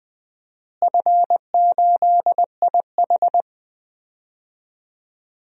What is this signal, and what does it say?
Morse code, heard as a keyed tone, about 20 wpm, 700 Hz. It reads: F8IH
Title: DXCC-100 Call Signs - 4 Characters - Repeated 2x - 20wpm